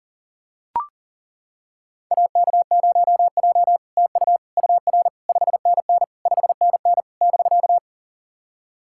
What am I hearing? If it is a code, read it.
Morse code, heard as a keyed tone, about 40 wpm, 700 Hz. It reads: AK0J TU UR 5NN 5NN <BK>
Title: SOTA QSO Practice 40wpm